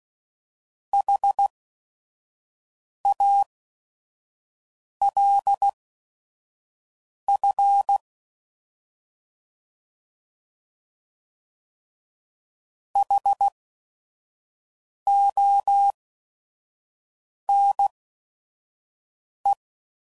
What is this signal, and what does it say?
Morse code, heard as a keyed tone, about 16 wpm, spaced as for 5 wpm, 800 Hz. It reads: HALF HONE
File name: Prefixes / By First Letter / Hs